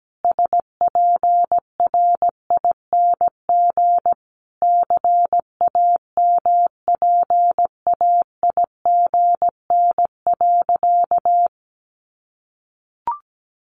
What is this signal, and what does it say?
Morse code, heard as a keyed tone, about 17 wpm, 700 Hz. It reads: SPRING CAMPAIGN.
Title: Sentences from Top 1200 Words 17wpm